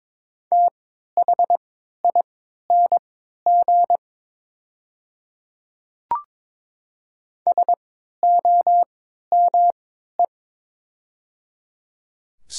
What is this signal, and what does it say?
Morse code, heard as a keyed tone, about 22 wpm, spaced as for 12 wpm, 700 Hz. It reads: THING SOME
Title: 3x Character Spacing - Top 200 Words - 22wpm